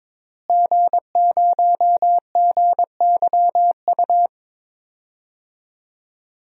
Morse code, heard as a keyed tone, about 21 wpm, 700 Hz. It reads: G0GYU
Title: DXCC-100 Call Signs - 5 Characters - Repeated 3x - 22wpm